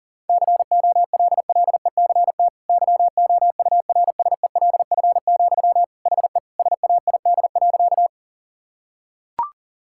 Morse code, heard as a keyed tone, about 40 wpm, 700 Hz. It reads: COLLECT YOURSELF, HE SAID.
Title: Sentences from Top 1600 Words 40wpm